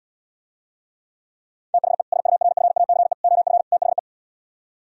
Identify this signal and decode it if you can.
Morse code, heard as a keyed tone, about 77 wpm, 700 Hz. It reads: THE SURFACE OF THE